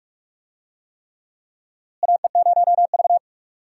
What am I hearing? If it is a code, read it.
Morse code, heard as a keyed tone, about 46 wpm, 700 Hz. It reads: AE0V
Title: USA Call Signs - 4 Characters - Repeated 3x - 45wpm